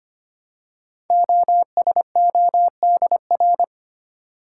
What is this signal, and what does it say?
Morse code, heard as a keyed tone, about 25 wpm, 700 Hz. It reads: OSODR
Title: Code Groups - 5 Letters - 25wpm